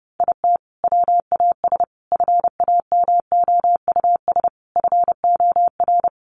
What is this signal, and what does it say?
Morse code, heard as a keyed tone, about 30 wpm, 700 Hz. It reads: IT WAS FAMOUS FOR
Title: Sentences from Top 1500 Words 30wpm